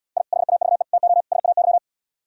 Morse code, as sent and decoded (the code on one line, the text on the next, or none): .. / .... .- ...- . / - --- / ... .- -.--
I HAVE TO SAY